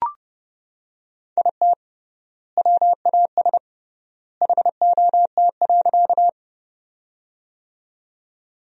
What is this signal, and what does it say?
Morse code, heard as a keyed tone, about 30 wpm, 700 Hz. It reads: IT WAS HOT.
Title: 3x Word-Spacing - Sentences from Top 500 Words - 30wpm